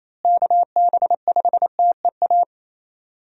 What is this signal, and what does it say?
Morse code, heard as a keyed tone, about 28 wpm, 700 Hz. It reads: KB5TEA
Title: USA Call Signs - 6 Characters - Repeated 3x - 28wpm